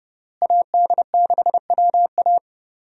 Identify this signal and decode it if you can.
Morse code, heard as a keyed tone, about 29 wpm, 700 Hz. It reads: AD6WA
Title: USA Call Signs - 5 Characters - Repeated 3x - 30wpm